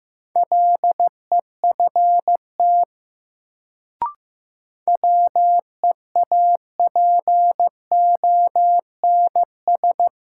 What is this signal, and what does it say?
Morse code, heard as a keyed tone, about 15 wpm, 700 Hz. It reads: LEFT WEAPONS
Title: Sets of 5 Words Encore - New Word Drill - 15wpm